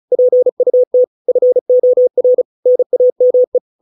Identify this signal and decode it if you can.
Morse code, heard as a keyed tone, about 34 wpm, 500 Hz. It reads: PUT FOR NAME